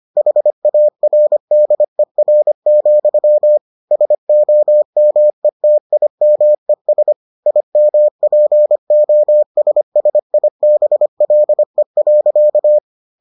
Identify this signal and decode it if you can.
Morse code, heard as a keyed tone, about 25 wpm, 600 Hz. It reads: HARDER, SOMETIMES IMPOSSIBLE.